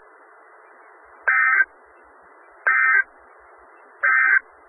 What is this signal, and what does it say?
Emergency Alert System (SAME) tones, heard as harsh buzzing bursts. It is heard from 1.2 to 4.4 s.
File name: Skywarn/2013/2013-03-18-Newton County WA4ASI Repeater